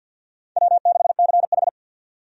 Morse code, heard as a keyed tone, about 49 wpm, 700 Hz. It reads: W6CH
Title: USA Call Signs - 4 Characters - 50wpm